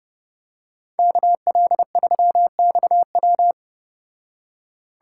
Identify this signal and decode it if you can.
Morse code, heard as a keyed tone, about 29 wpm, 700 Hz. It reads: KL3XW